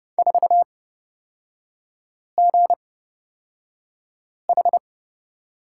Morse code, heard as a keyed tone, about 31 wpm, spaced as for 5 wpm, 700 Hz. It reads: 4GH
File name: Lesson 37e - Letter x - Three Characters 30wpm